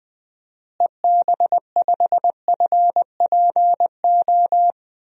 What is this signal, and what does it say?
Morse code, heard as a keyed tone, about 20 wpm, 700 Hz. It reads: EB5FPO
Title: DXCC-100 Call Signs - 6 Characters - Repeated 2x - 20wpm